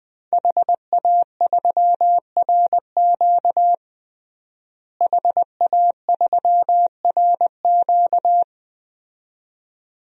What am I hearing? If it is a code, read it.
Morse code, heard as a keyed tone, about 21 wpm, 700 Hz. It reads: HA3RQ HA3RQ
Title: DXCC-100 Call Signs - 5 Characters - Repeated 2x - 20wpm